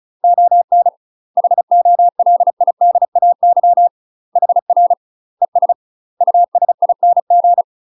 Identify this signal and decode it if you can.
Morse code, heard as a keyed tone, about 35 wpm, 700 Hz. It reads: ON HOLIDAY HR ES USING
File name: UK MASTER QSOs TRAINING / 35WPM / UK MASTER QSOs 0038